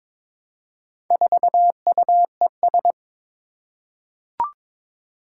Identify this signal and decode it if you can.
Morse code, heard as a keyed tone, about 22 wpm, 700 Hz. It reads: 4UES
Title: Four Challenging Letter-Number Combinations 22wpm